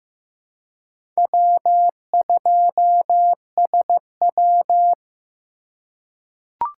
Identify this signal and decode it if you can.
Morse code, heard as a keyed tone, about 15 wpm, 700 Hz. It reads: W2SW
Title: USA Call Signs - 4 Characters - 4x Character Spacing - 15wpm